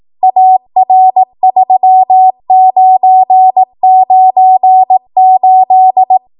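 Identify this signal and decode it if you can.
Morse code, heard as a keyed tone, about 18 wpm, 750 Hz. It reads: AR3998
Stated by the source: Code practice at 18 WPM, a transition run bridging 15 to 20 WPM.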